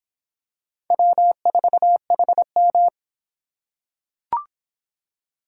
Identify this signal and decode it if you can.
Morse code, heard as a keyed tone, about 23 wpm, 700 Hz. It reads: W4HM